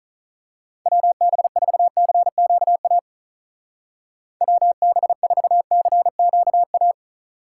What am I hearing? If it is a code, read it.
Morse code, heard as a keyed tone, about 39 wpm, 700 Hz. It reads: WB4CQA WB4CQA
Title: Speed Racing - USA 2x3 Call Signs 35wpm